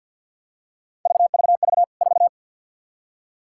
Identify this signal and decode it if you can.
Morse code, heard as a keyed tone, about 50 wpm, 700 Hz. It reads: VVV 4
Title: K1USN SST QSO Practice 50wpm